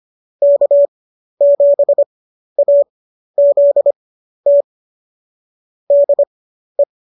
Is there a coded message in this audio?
Yes: Morse code from 0.4 to 6.8 s, heard as a keyed tone, about 25 wpm, spaced as for 12 wpm, 590 Hz. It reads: K7AZT DE